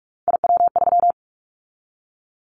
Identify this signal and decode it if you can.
Morse code, heard as a keyed tone, about 46 wpm, 700 Hz. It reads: IW3
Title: Callsign Prefixes for Top 25 DXCC - 45wpm